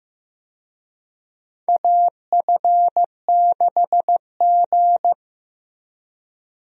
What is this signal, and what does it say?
Morse code, heard as a keyed tone, about 15 wpm, 700 Hz. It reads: AF6G
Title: USA Call Signs - 4 Characters - Repeated 3x - 15wpm